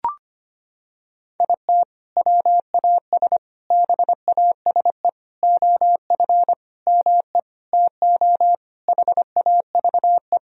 Morse code, heard as a keyed tone, about 25 wpm, 700 Hz. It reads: IT WAS BASE OF ME TO HAVE
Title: Sentences from Top 1000 Words - Review Part 1 - 25wpm